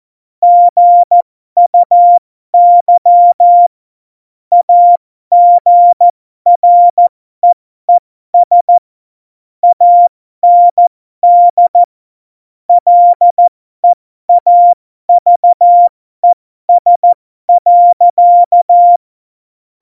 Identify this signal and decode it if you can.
Morse code, heard as a keyed tone, about 14 wpm, spaced as for 12 wpm, 700 Hz. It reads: GUY AGREES AND LEAVES.
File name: Morsekurs / Witze